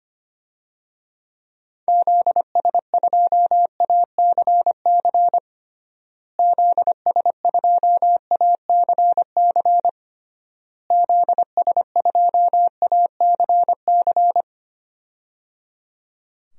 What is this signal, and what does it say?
Morse code, heard as a keyed tone, about 25 wpm, 700 Hz. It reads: ZS2ACC ZS2ACC ZS2ACC
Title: DXCC-100 Call Signs - 6 Characters - Repeated 3x - 25wpm